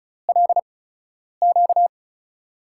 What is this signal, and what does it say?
Morse code, heard as a keyed tone, about 36 wpm, spaced as for 10 wpm, 700 Hz. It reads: LQ
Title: USA Call Signs - 4 Characters - 8x Character Spacing - 35wpm